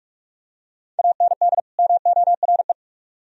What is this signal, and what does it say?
Morse code, heard as a keyed tone, about 44 wpm, 700 Hz. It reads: AND MORE